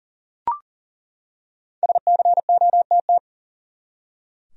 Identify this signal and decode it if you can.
Morse code, heard as a keyed tone, about 40 wpm, 700 Hz. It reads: SCOTT
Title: USA Common Names - Max 5 Letters - 40wpm